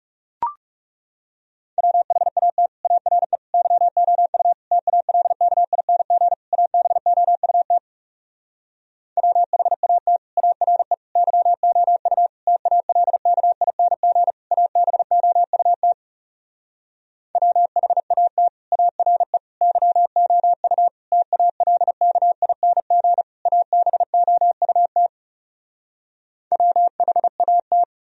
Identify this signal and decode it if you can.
Morse code, heard as a keyed tone, about 40 wpm, 700 Hz. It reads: WHAT ARE YOU TALKING ABOUT WHAT ARE YOU TALKING ABOUT WHAT ARE YOU TALKING ABOUT WHAT